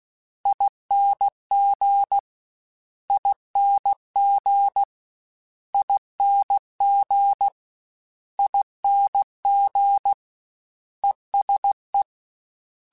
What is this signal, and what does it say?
Morse code, heard as a keyed tone, about 16 wpm, 800 Hz. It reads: ING ING ING ING ESE
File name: Special INGs-mp3s / Special INGs-mp3s / ESE ING @15